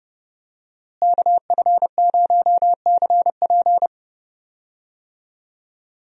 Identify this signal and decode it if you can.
Morse code, heard as a keyed tone, about 30 wpm, 700 Hz. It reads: KF0CP
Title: USA Call Signs - 5 Characters - Repeated 3x - 30wpm